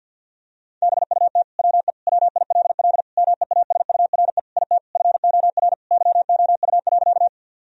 Morse code, heard as a keyed tone, about 50 wpm, 700 Hz. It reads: BUT WE WILL MEASURE IT FOR YOU.
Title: Sentences from Top 1000 Words - Review Part 2 - 50wpm